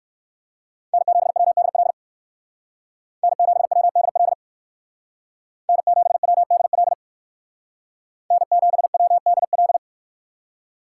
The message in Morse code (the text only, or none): N7WDL N7WDL N7WDL N7WDL